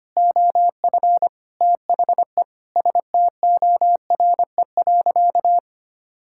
Morse code, heard as a keyed tone, about 25 wpm, 700 Hz. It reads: OF THE STORE.
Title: Sentences from Top 1000 Words - Review Part 5 - 25wpm